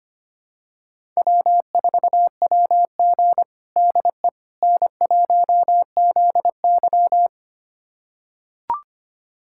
Morse code, heard as a keyed tone, about 25 wpm, 700 Hz. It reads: W4WG DE N1ZY
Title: USA Callsign DE Callsign - 4 Characters - 25wpm